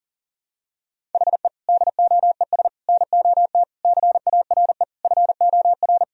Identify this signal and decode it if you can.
Morse code, heard as a keyed tone, about 40 wpm, 700 Hz. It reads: HE DOES NOT CARE FOR
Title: Sentences from Top 1000 Words - Review Part 1 - 40wpm